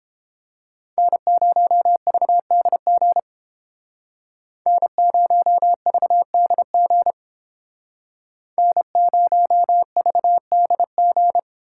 Morse code, heard as a keyed tone, about 31 wpm, 700 Hz. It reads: N0VDG N0VDG N0VDG